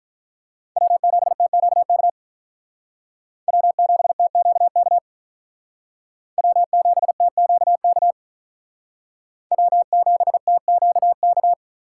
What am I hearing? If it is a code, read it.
Morse code, heard as a keyed tone, about 45 wpm, 700 Hz. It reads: W7TQK W7TQK W7TQK W7TQK